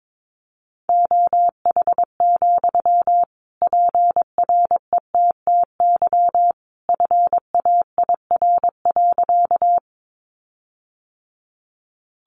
Morse code, heard as a keyed tone, about 22 wpm, 700 Hz. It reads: OH, PRETTY FAIR.
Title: Sentences from Top 1300 Words 22wpm